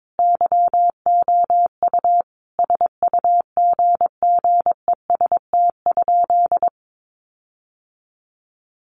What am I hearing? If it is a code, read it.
Morse code, heard as a keyed tone, about 22 wpm, 700 Hz. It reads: YOU SUGGEST?
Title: Sentences from Top 1300 Words 22wpm